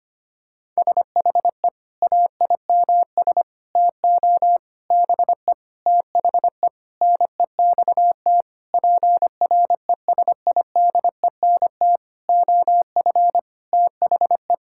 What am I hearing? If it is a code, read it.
Morse code, heard as a keyed tone, about 25 wpm, 700 Hz. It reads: SHE AIMS TO BE THE NEXT PRESIDENT OF THE
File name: practice / 25wpm / Sentences with 9-Word Phrases - 25wpm